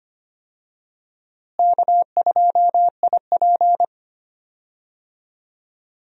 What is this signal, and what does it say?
Morse code, heard as a keyed tone, about 25 wpm, 700 Hz. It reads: K2IP
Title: USA Call Signs - 4 Characters - Repeated 3x - 25wpm